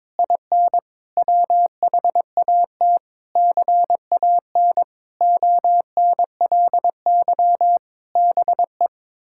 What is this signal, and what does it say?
Morse code, heard as a keyed tone, about 22 wpm, 700 Hz. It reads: IN WHAT CAN ONLY BE
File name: practice / 22wpm / Sentences with 8-Word Phrases - 22wpm